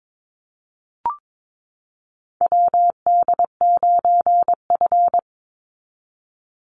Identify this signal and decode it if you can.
Morse code, heard as a keyed tone, about 22 wpm, 700 Hz. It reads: WD9F
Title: NA SOTA Chaser Callsigns 22wpm